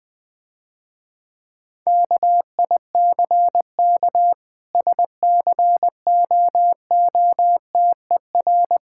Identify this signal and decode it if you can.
Morse code, heard as a keyed tone, about 20 wpm, 700 Hz. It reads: KICK SCOOTER
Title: Difficult Words - Speed-Racing with Hints - 20wpm